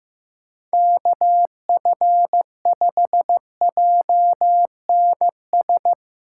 Morse code, heard as a keyed tone, about 15 wpm, 700 Hz. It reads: KF5JNS
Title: USA Callsign DE Callsign - 6 Characters - 15wpm